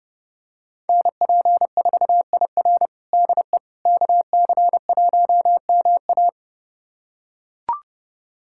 Morse code, heard as a keyed tone, about 31 wpm, 700 Hz. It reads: NP4IR DE KC1MA
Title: USA Callsign DE Callsign - 5 Characters - 30wpm